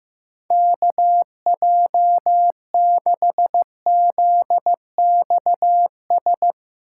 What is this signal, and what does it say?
Morse code, heard as a keyed tone, about 15 wpm, 700 Hz. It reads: KJ6ZXS
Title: USA Call Signs - 6 Characters - 2x Character Spacing - 15wpm